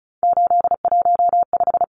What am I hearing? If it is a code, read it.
Morse code, heard as a keyed tone, about 34 wpm, 700 Hz. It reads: 815
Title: Numbers - 3-Digit 35wpm